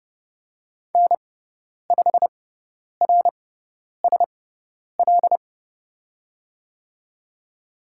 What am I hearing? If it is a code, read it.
Morse code, heard as a keyed tone, about 30 wpm, spaced as for 10 wpm, 700 Hz. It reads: N5RSL
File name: Lesson 17g - Letter u - Call Signs 30wpm